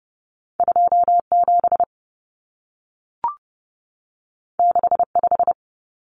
Morse code, heard as a keyed tone, about 30 wpm, 700 Hz. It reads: 27 65
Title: Numbers - 2-Digit 30wpm